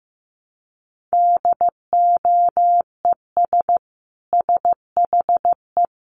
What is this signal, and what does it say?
Morse code, heard as a keyed tone, about 15 wpm, 700 Hz. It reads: DOES SHE
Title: Sentences from Top 1400 Words 15wpm